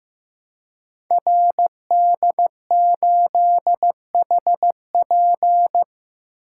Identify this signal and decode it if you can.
Morse code, heard as a keyed tone, about 15 wpm, 700 Hz. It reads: RD8HP
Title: Code Groups - 5 Letter:Numbers - 15wpm